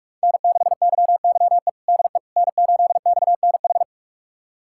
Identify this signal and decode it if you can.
Morse code, heard as a keyed tone, about 46 wpm, 700 Hz. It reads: N6YYE DE N8XNH
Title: USA Callsign DE Callsign - 5 Characters - 45wpm